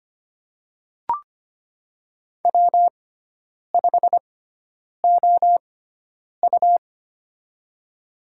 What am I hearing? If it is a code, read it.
Morse code, heard as a keyed tone, about 25 wpm, spaced as for 9 wpm, 700 Hz. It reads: W5OU
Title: USA Call Signs - 4 Characters - 6x Character Spacing - 25wpm